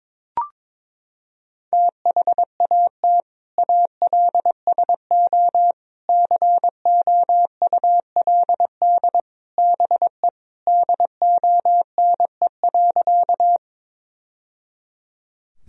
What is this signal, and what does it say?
Morse code, heard as a keyed tone, about 22 wpm, 700 Hz. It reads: THAT ALSO COULD BE DONE.
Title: Sentences from Top 1000 Words - Review Part 1 - 22wpm